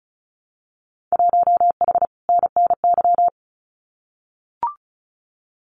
Morse code, heard as a keyed tone, about 36 wpm, 700 Hz. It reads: 1H NNY
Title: Winter Field Day Exchanges 35wpm